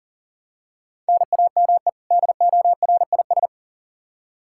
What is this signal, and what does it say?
Morse code, heard as a keyed tone, about 40 wpm, 700 Hz. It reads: NAME DORIS
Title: Practice QSO - Exchange 1 - Beginning - 40wpm